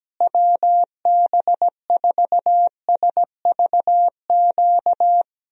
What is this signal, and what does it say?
Morse code, heard as a keyed tone, about 17 wpm, 700 Hz. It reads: WB4SVQ
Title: USA Call Signs - 6 Characters - Repeated 3x - 17wpm